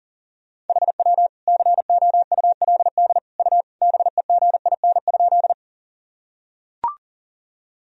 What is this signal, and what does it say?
Morse code, heard as a keyed tone, about 40 wpm, 700 Hz. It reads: HW COULD U BEGIN?
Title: Sentences from Top 300 Words and Abbreviations 40wpm